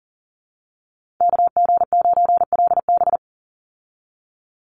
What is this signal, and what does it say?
Morse code, heard as a keyed tone, about 40 wpm, 700 Hz. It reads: KG9LB
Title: Call Signs Repeated 2x - 40wpm